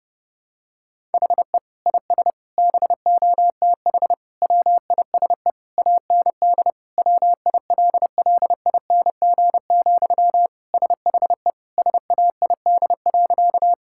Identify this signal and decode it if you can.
Morse code, heard as a keyed tone, about 30 wpm, 700 Hz. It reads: HE IS BOTH WISE AND WILLING, SHE SAID.
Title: Sentences from Top 1600 Words 30wpm